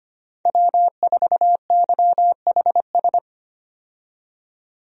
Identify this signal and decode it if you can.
Morse code, heard as a keyed tone, about 25 wpm, 700 Hz. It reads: W4YHS
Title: USA Call Signs - 5 Characters - 25wpm